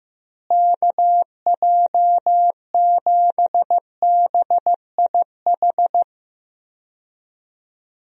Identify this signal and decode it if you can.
Morse code, heard as a keyed tone, about 15 wpm, 700 Hz. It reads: KJ7BIH
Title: USA Call Signs - 6 Characters - Repeated 3x - 15wpm